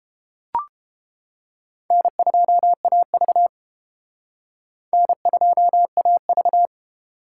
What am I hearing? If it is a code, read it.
Morse code, heard as a keyed tone, about 29 wpm, 700 Hz. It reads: N2AV N2AV